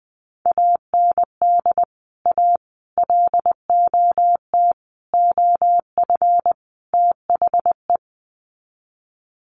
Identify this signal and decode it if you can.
Morse code, heard as a keyed tone, about 20 wpm, 700 Hz. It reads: AND A LOT OF THE
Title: Sets of 5 Words Encore 20wpm